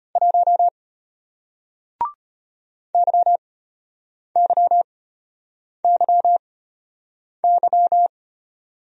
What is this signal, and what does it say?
Morse code, heard as a keyed tone, about 33 wpm, spaced as for 8 wpm, 700 Hz. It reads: 1 YYYY